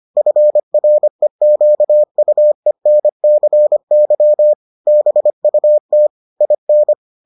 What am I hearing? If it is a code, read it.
Morse code, heard as a keyed tone, about 25 wpm, 600 Hz. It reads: FREQUENCY BUT IN